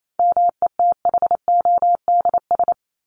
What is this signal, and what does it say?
Morse code, heard as a keyed tone, about 28 wpm, 700 Hz. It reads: METHODS
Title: ES:S Suffix - Top 100 Words with Base Words 28wpm